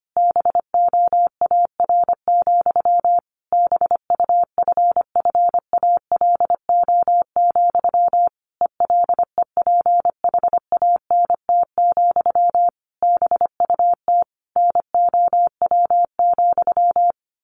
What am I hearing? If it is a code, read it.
Morse code, heard as a keyed tone, about 25 wpm, 700 Hz. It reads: BOAR, BUFFALO, ELEPHANT, BUT NOW,
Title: A Sound Of Thunder 25wpm